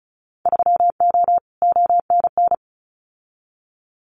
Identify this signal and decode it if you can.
Morse code, heard as a keyed tone, about 34 wpm, 700 Hz. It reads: 3O ONN
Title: Winter Field Day Exchanges 35wpm